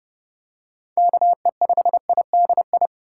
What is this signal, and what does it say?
Morse code, heard as a keyed tone, about 30 wpm, 700 Hz. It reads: KE5IDI
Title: USA Call Signs - 6 Characters - 2x Character Spacing - 30wpm